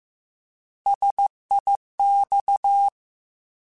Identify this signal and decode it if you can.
Morse code, heard as a keyed tone, about 15 wpm, 800 Hz. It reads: SIX
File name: Alphabetical / Alphabetical mp3s / Ss @13